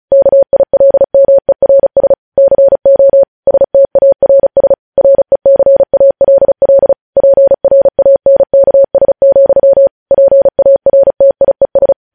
Morse code, heard as a keyed tone, about 36 wpm, 550 Hz. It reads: KILMERS CO STARS RECALL PRANKS, PARTIES
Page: news headlines in morse code at 35 wpm